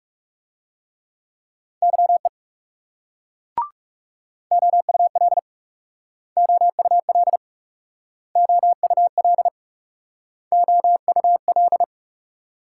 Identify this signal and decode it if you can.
Morse code, heard as a keyed tone, about 40 wpm, 700 Hz. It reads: YE OUL OUL OUL OUL